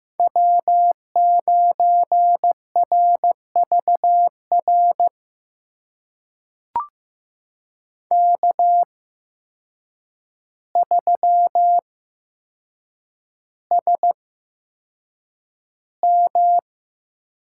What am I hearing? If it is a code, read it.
Morse code, heard as a keyed tone, about 15 wpm, spaced as for 7 wpm, 700 Hz. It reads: W9RVR K3SM
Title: USA Call Signs - 8x Character Spacing - 15wpm